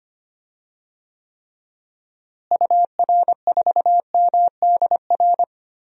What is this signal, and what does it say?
Morse code, heard as a keyed tone, about 25 wpm, 700 Hz. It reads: UR4MDR
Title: DXCC-100 Call Signs - 6 Characters - Repeated 3x - 25wpm